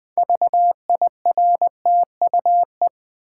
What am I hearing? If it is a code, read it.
Morse code, heard as a keyed tone, about 20 wpm, 700 Hz. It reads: VIRTUE
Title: Build A Word - 3 Letters At A Time - 6-Letter Words - 20wpm